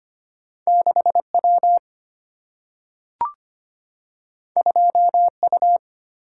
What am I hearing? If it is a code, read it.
Morse code, heard as a keyed tone, about 25 wpm, 700 Hz. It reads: 6W 2U
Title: Single-Letter Single-Number 25wpm